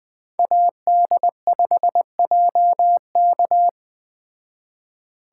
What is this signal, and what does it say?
Morse code, heard as a keyed tone, about 20 wpm, 700 Hz. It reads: AD5JK
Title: Speed Racing - USA 2x2 Call Signs 17wpm